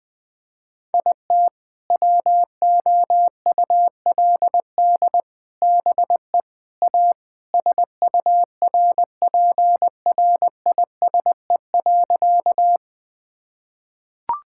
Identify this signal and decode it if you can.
Morse code, heard as a keyed tone, about 19 wpm, 700 Hz. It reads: IT WOULD BE A SURPRISE.
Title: Sentences with 4-Word Phrases - 20wpm